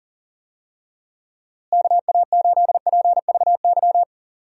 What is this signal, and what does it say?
Morse code, heard as a keyed tone, about 39 wpm, 700 Hz. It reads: KA8PVY
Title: USA Call Signs - 6 Characters - Repeated 3x - 40wpm